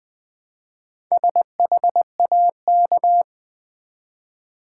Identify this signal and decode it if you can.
Morse code, heard as a keyed tone, about 20 wpm, 700 Hz. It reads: SHAK
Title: Build A Word - 1 Letter At A Time - 5-Letter Words - 20wpm